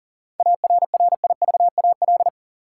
Morse code, heard as a keyed tone, about 40 wpm, 700 Hz. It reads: ARRIVAL
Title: Seven Letter Words - 40wpm